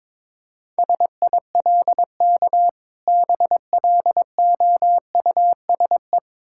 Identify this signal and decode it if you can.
Morse code, heard as a keyed tone, about 22 wpm, 690 Hz. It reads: SILK BLOUSE
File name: Difficult Words with Hints - 22wpm